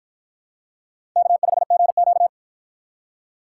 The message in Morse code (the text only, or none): K5GQ